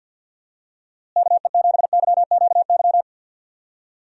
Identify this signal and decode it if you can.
Morse code, heard as a keyed tone, about 46 wpm, 700 Hz. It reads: KE7YQY
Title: USA Call Signs - 6 Characters - Repeated 3x - 50wpm